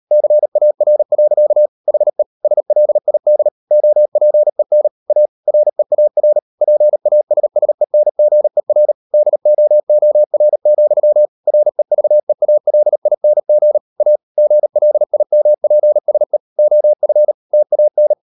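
Morse code, heard as a keyed tone, about 38 wpm, 600 Hz. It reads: CAR. HE SLID OPEN A REAR PASSENGER DOOR, REVEALING A GLIMPSE OF TAN
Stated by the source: Morse code practice at 38 WPM, sending Ken Follett's novel Never in English, from CW med Gnister.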